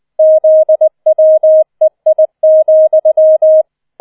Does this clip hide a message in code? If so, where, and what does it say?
Yes: Morse code from 0.2 to 3.6 s, heard as a keyed tone, about 20 wpm, 630 Hz. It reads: ZWEI,